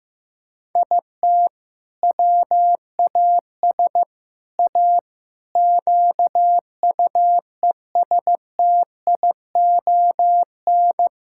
Morse code, heard as a keyed tone, about 16 wpm, 700 Hz. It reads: IT WAS A QUESTION
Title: Sentences from Top 1000 Words - Review Part 6 - 15wpm